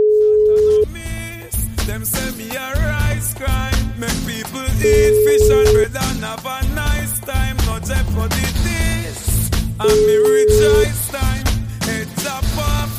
Ringback tone (Europe, 425 Hz), with noise.